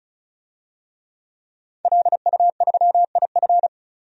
Morse code, heard as a keyed tone, about 36 wpm, 700 Hz. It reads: LU3IF
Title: DXCC-100 Call Signs - 5 Characters - 35wpm